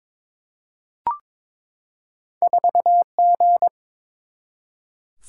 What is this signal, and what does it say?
Morse code, heard as a keyed tone, about 22 wpm, 700 Hz. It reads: 4G
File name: practice / 22wpm / Two Letter-Number Combinations - 22wpm